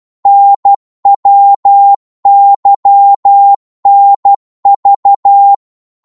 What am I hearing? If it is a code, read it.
Morse code, heard as a keyed tone, about 12 wpm, 800 Hz. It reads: NWYNV
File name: Morse / ltr-12wpm